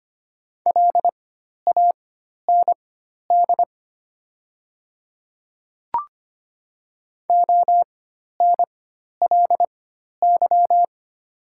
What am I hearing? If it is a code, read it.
Morse code, heard as a keyed tone, about 25 wpm, spaced as for 12 wpm, 700 Hz. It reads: LAND ONLY
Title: 4x Character Spacing - Top 300 Words - 25wpm